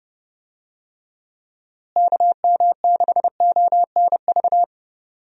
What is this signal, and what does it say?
Morse code, heard as a keyed tone, about 30 wpm, 700 Hz. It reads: KM6ONV
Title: USA Call Signs - 6 Characters - Repeated 3x - 30wpm